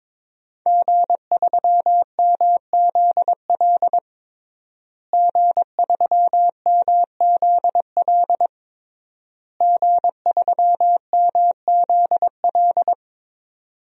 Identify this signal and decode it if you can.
Morse code, heard as a keyed tone, about 22 wpm, 700 Hz. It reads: G3MZL G3MZL G3MZL
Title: DXCC-100 Call Signs - 5 Characters - Repeated 3x - 22wpm